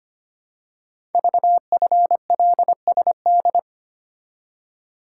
Morse code, heard as a keyed tone, about 25 wpm, 700 Hz. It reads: VFLSD
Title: Code Groups - 5 Letters - 25wpm